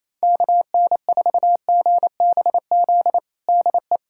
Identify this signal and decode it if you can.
Morse code, heard as a keyed tone, about 28 wpm, 700 Hz. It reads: KN4GBZ DE
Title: USA Callsign DE Callsign 28wpm